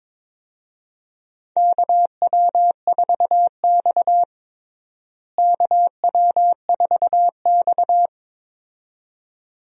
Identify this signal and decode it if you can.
Morse code, heard as a keyed tone, about 23 wpm, 700 Hz. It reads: KW4X KW4X
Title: USA Call Signs - 4 Characters - Repeated 2x - 22wpm